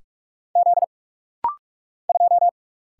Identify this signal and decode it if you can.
Morse code, heard as a keyed tone, about 46 wpm, 700 Hz. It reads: Z 2
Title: Single Letter-Number Alt - Rapid-Fire 45wpm